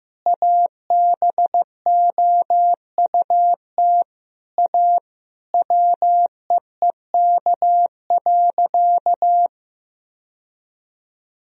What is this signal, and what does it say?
Morse code, heard as a keyed tone, about 15 wpm, 700 Hz. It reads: ABOUT A WEEK.
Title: Sentences from Top 1000 Words - Review Part 2 - 15wpm